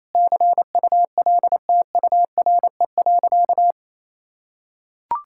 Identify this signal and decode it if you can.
Morse code, heard as a keyed tone, about 27 wpm, 700 Hz. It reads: CULTURE.
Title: Sentences from Top 1000 Words - Review Part 6 - 28wpm